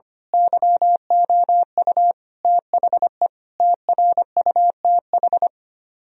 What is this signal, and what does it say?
Morse code, heard as a keyed tone, about 24 wpm, 700 Hz. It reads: YOU THE TRUTH
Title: Speed-Racing - Sets of 5 Words 25wpm